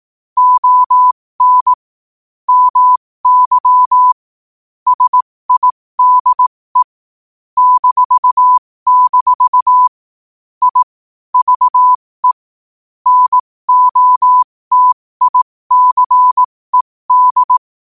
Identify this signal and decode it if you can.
Morse code, heard as a keyed tone, about 18 wpm, spaced as for 15 wpm, 1000 Hz. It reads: ON MY SIDE -- I VE NOTICED
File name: qotdfiles / 15wpm